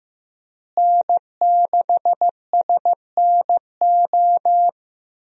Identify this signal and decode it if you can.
Morse code, heard as a keyed tone, about 16 wpm, 700 Hz. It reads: N6SNO
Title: USA Call Signs - 5 Characters - Repeated 2x - 15wpm